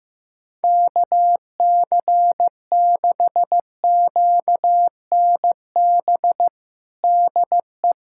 Morse code, heard as a keyed tone, about 15 wpm, 700 Hz. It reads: KC6QNB DE
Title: USA Callsign DE Callsign - 6 Characters - 15wpm